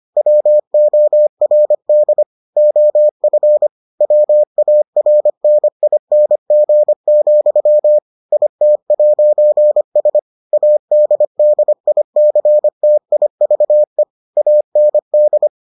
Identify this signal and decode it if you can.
Morse code, heard as a keyed tone, about 25 wpm, 600 Hz. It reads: WORD OF WARNING, IT'S ADDICTIVE AND